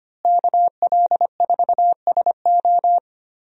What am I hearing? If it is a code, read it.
Morse code, heard as a keyed tone, about 25 wpm, 700 Hz. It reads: KL4SO
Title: USA Call Signs - 5 Characters - Repeated 3x - 25wpm